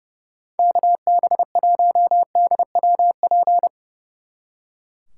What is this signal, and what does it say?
Morse code, heard as a keyed tone, about 36 wpm, 700 Hz. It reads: KB1DWP
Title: Speed Racing - USA Call Signs 30wpm